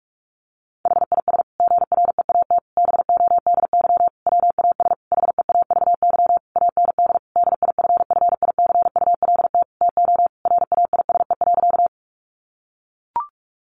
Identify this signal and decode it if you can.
Morse code, heard as a keyed tone, about 44 wpm, 700 Hz. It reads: HIS GREAT BODY WAS HEAVY AND DIFFICULT TO RAISE.
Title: Sentences from Top 1400 Words 45wpm